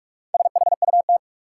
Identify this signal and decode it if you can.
Morse code, heard as a keyed tone, about 47 wpm, 700 Hz. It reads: SHUT
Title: Four Letter Words - 45wpm